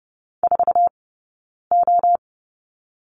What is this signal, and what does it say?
Morse code, heard as a keyed tone, about 31 wpm, 700 Hz. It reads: 4 O
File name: Lesson 17d - Letter u - Two Characters 30wpm